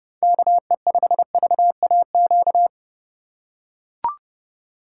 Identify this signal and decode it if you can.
Morse code, heard as a keyed tone, about 32 wpm, 700 Hz. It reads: KE5VAQ
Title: USA Call Signs - 6 Characters - 2x Character Spacing - 30wpm